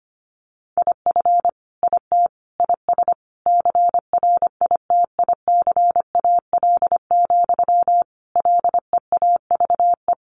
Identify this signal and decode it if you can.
Morse code, heard as a keyed tone, about 25 wpm, 700 Hz. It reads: IF IT IS CRITICAL, LEAVE
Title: Sentences from Top 1100 Words 25wpm